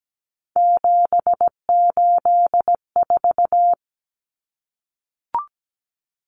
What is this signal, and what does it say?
Morse code, heard as a keyed tone, about 17 wpm, 700 Hz. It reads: 784
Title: Common Cut Numbers - 3-Digit 17wpm